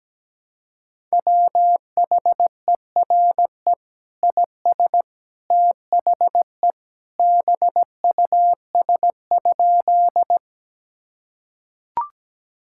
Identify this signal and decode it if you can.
Morse code, heard as a keyed tone, about 17 wpm, 700 Hz. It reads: WHERE IS THE BUS?
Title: Sentences from Top 1000 Words - Review Part 6 - 17wpm